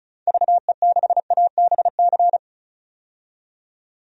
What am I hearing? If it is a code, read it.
Morse code, heard as a keyed tone, about 36 wpm, 700 Hz. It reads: VE6ABC
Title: DXCC-100 Call Signs - 6 Characters - 35wpm